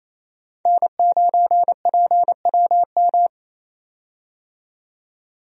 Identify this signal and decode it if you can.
Morse code, heard as a keyed tone, about 28 wpm, 700 Hz. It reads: N9PWM
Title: USA Call Signs - 5 Characters - 28wpm